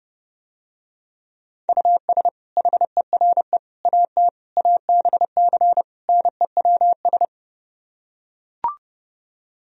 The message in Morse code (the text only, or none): US HERE AT ABC NEWS